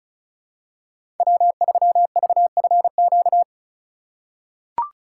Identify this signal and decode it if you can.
Morse code, heard as a keyed tone, about 34 wpm, 700 Hz. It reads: W3VFQ
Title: USA Call Signs - 5 Characters - 2x Character Spacing - 35wpm